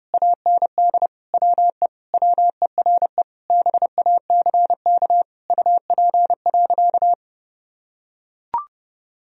Morse code, heard as a keyed tone, about 30 wpm, 700 Hz. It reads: AND WE WERE BACK UP.
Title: Sentences from Top 1000 Words - Review Part 2 - 30wpm